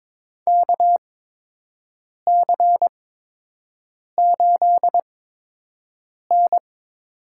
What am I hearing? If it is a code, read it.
Morse code, heard as a keyed tone, about 22 wpm, spaced as for 6 wpm, 700 Hz. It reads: KC8N
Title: USA Call Signs - 6 Characters - 8x Character Spacing - 22wpm